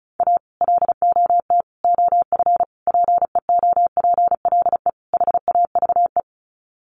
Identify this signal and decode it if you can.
Morse code, heard as a keyed tone, about 34 wpm, 700 Hz. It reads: A LOT OF PEOPLE HAVE
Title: Sets of 5 Words 35wpm